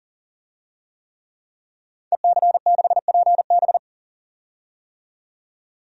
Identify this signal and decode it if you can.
Morse code, heard as a keyed tone, about 40 wpm, 700 Hz. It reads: EC6PB
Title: DXCC-100 Call Signs - 5 Characters - 40wpm